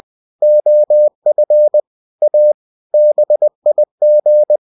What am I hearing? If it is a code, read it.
Morse code, heard as a keyed tone, about 20 wpm, 600 Hz. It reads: OF A BIG
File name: cw-books / winnie-the-pooh / winnie-20wpm-page-0024